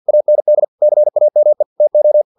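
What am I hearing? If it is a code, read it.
Morse code, heard as a keyed tone, about 50 wpm, 600 Hz. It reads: AND CAME TO
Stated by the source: Morse practice text from an English-language novel at 49 wpm.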